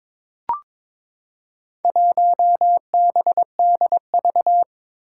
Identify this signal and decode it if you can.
Morse code, heard as a keyed tone, about 22 wpm, 700 Hz. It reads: 1BDV
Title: Four Challenging Letter-Number Combinations 22wpm